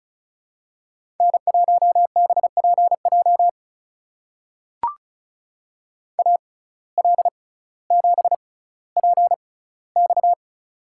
Morse code, heard as a keyed tone, about 36 wpm, spaced as for 18 wpm, 700 Hz. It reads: N1BPJ AL7PX
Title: USA Call Signs - 5 Characters - 6x Character Spacing - 35wpm